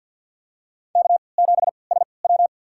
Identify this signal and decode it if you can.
Morse code, heard as a keyed tone, about 50 wpm, spaced as for 28 wpm, 700 Hz. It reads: K7SW
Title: USA Call Signs - 4 Characters - 3x Character Spacing - 50wpm